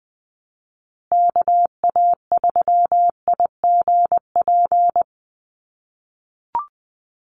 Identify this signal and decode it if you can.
Morse code, heard as a keyed tone, about 20 wpm, 700 Hz. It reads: KA3IGP
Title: USA Call Signs Repeated 2x 20wpm